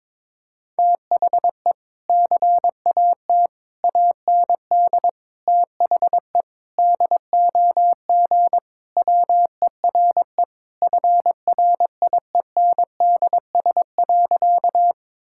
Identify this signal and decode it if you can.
Morse code, heard as a keyed tone, about 22 wpm, 700 Hz. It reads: THE CAT AND THE DOG WERE FRIENDS.
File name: practice / 22wpm / Sentences with 2-Word Phrases - 22wpm